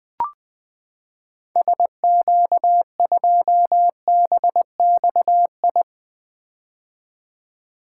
Morse code, heard as a keyed tone, about 20 wpm, 700 Hz. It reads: SQ2BXI
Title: 2023 POTA Activator Call Signs - 20wpm